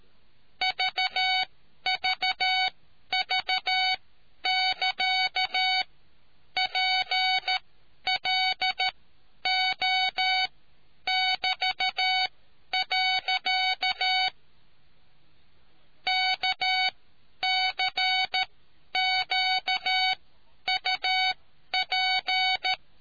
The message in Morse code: VVV <KA> PLO <BT>. KCQUP

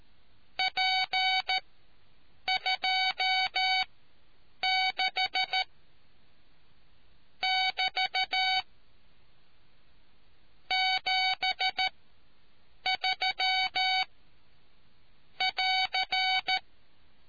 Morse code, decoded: P26 <BT> 73 <AR>

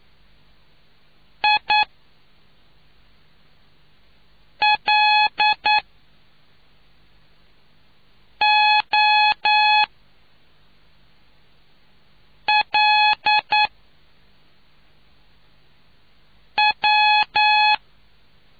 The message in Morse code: I L O L W

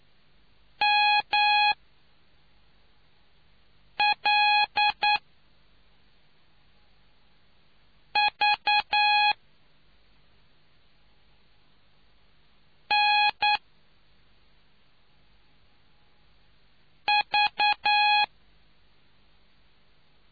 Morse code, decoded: MLV N V

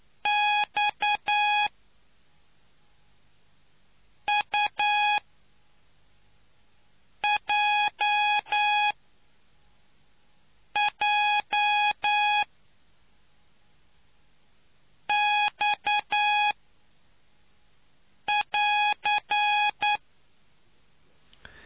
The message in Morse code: XUJJX <AR>

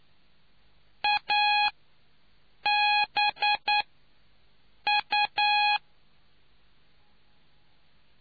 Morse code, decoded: A B U

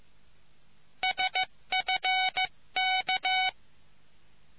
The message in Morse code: SFK

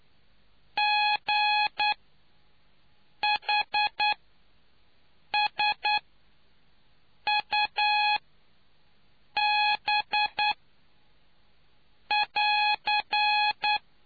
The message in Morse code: GHSUB <AR>

